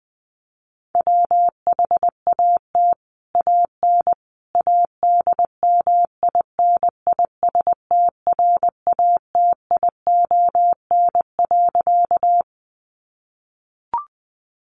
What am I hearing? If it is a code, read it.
Morse code, heard as a keyed tone, about 20 wpm, 700 Hz. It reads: WHAT AN ADMINISTRATION.
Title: Sentences from Top 1300 Words 20wpm